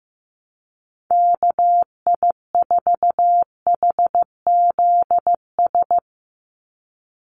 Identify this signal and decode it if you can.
Morse code, heard as a keyed tone, about 15 wpm, 700 Hz. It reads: KI4HZS
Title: USA Call Signs 15wpm